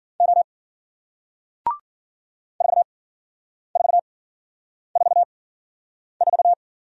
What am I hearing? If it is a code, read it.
Morse code, heard as a keyed tone, about 53 wpm, spaced as for 9 wpm, 700 Hz. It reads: O 4444